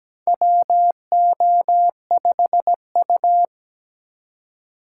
Morse code, heard as a keyed tone, about 17 wpm, 700 Hz. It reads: WO5U
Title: USA Call Signs - 4 Characters - 3x Character Spacing - 17wpm